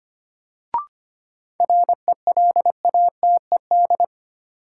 Morse code, heard as a keyed tone, about 25 wpm, 700 Hz. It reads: RELATED
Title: Sentences from Top 1400 Words - New Word Drill Alt - 25wpm